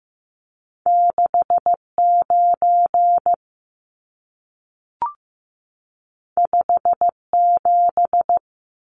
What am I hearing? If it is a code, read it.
Morse code, heard as a keyed tone, about 15 wpm, 700 Hz. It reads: 6 9 5 7
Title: Numbers - 2-Digit 15wpm